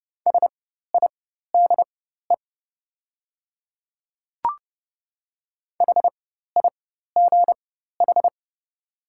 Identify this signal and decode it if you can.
Morse code, heard as a keyed tone, about 30 wpm, spaced as for 14 wpm, 700 Hz. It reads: SIDE HIGH
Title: 4x Character Spacing - Top 200 Words - 30wpm